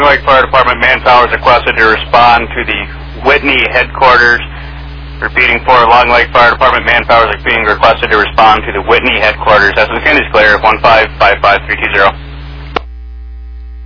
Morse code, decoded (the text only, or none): none